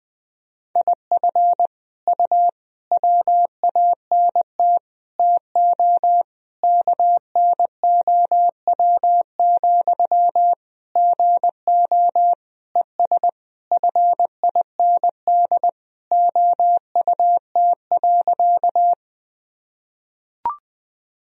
Morse code, heard as a keyed tone, about 20 wpm, 700 Hz. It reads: IF U WANT TO KNOW, GO ES FIND OUT.
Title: Sentences from Top 200 Words and Abbreviations 20wpm